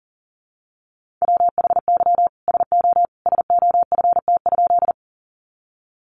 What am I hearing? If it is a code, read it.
Morse code, heard as a keyed tone, about 40 wpm, 700 Hz. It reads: WHY SO SOFT?
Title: Sentences from Top 1500 Words 40wpm